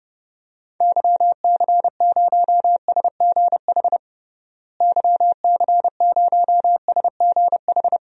Morse code, heard as a keyed tone, about 30 wpm, 700 Hz. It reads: YC0SGH YC0SGH
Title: DXCC-100 Call Signs - 6 Characters - Repeated 3x - 30wpm